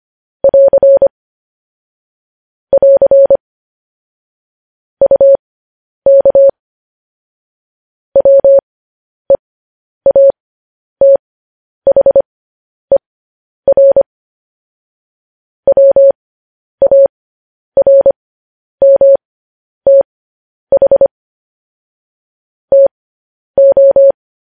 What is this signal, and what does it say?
Morse code, heard as a keyed tone, about 25 wpm, spaced as for 10 wpm, 550 Hz. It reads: <AR> <AR> UK WEATHER WARMTH TO
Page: news headlines in morse code at 10 wpm